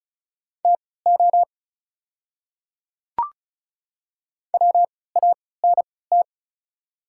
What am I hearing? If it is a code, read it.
Morse code, heard as a keyed tone, about 36 wpm, spaced as for 20 wpm, 690 Hz. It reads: TO WANT
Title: 3x Character Spacing - Top 200 Words - 35wpm